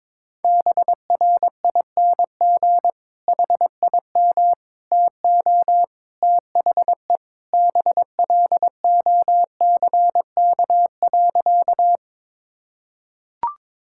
Morse code, heard as a keyed tone, about 22 wpm, 700 Hz. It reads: BRING HIM TO THE BLOCK.
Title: Sentences from Top 1600 Words 22wpm